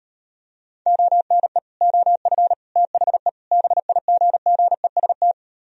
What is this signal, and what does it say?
Morse code, heard as a keyed tone, about 38 wpm, 700 Hz. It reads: ONE OF THE BIGGEST